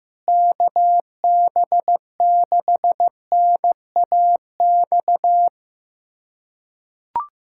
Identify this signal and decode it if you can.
Morse code, heard as a keyed tone, about 15 wpm, 700 Hz. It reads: KB6NAX
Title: USA Callsign DE Callsign - 6 Characters - 15wpm